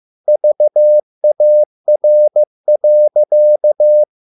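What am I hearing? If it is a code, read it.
Morse code, heard as a keyed tone, about 15 wpm, 600 Hz. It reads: VAR.
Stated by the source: Morse code at a slow 15 wpm from CW med Gnister.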